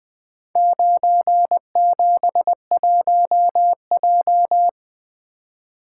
Morse code, heard as a keyed tone, about 20 wpm, 700 Hz. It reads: 971J
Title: Four Letter-Number Combinations - 20wpm